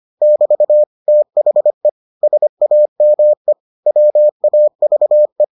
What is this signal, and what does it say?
Morse code, heard as a keyed tone, about 25 wpm, 600 Hz. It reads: <BT> THE SAME WAVE